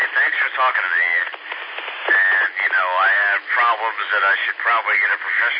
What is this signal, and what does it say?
Emergency Alert System (SAME) tones, heard as harsh buzzing bursts, with speech. It is heard from 0.8 to 4.4 s.